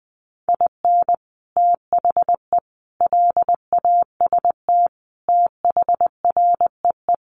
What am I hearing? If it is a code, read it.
Morse code, heard as a keyed tone, about 20 wpm, 700 Hz. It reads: IN THE LAST THREE
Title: Sets of 5 Words Encore2 20wpm